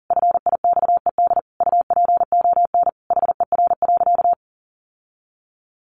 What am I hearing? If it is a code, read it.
Morse code, heard as a keyed tone, about 40 wpm, 700 Hz. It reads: FIXED UPON HER.
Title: Sentences from Top 1100 Words 40wpm